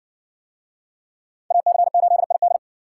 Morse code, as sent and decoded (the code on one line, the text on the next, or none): .- -.-. ---.. .. -..
AC8ID